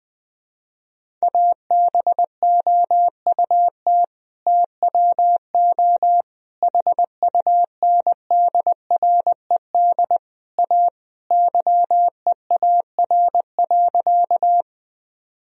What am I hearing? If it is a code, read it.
Morse code, heard as a keyed tone, about 21 wpm, 700 Hz. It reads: ABOUT TWO HUNDRED A YEAR.
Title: Sentences from Top 1000 Words - Review Part 2 - 20wpm